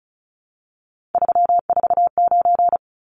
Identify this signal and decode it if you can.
Morse code, heard as a keyed tone, about 34 wpm, 700 Hz. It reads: 349
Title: Numbers - 3-Digit 35wpm